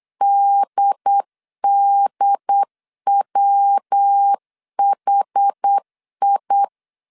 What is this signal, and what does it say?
Morse code, heard as a keyed tone, about 8 wpm, 800 Hz. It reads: DDWHI
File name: CW / DET / 1 3 1500C S B 8wpm 4141